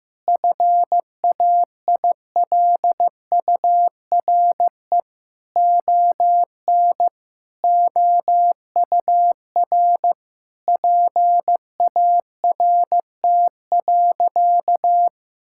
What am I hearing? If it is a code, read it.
Morse code, heard as a keyed tone, about 15 wpm, 700 Hz. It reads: FAILURE ON OUR PART.
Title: Sentences from Top 1000 Words - Review Part 6 - 15wpm